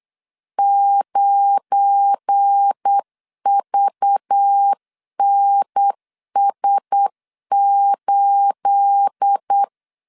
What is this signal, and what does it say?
Morse code, heard as a keyed tone, about 8 wpm, 800 Hz. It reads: 9VNS8
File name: LEZIONI / 9 28 8wpm 3512